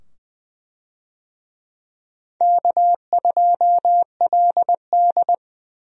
Morse code, heard as a keyed tone, about 20 wpm, 700 Hz. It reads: K2LD
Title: USA Call Signs - 4 Characters - 20wpm